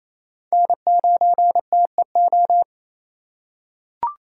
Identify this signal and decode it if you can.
Morse code, heard as a keyed tone, about 28 wpm, 700 Hz. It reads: N9TEO
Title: USA Call Signs - 5 Characters - 6x Character Spacing - 28wpm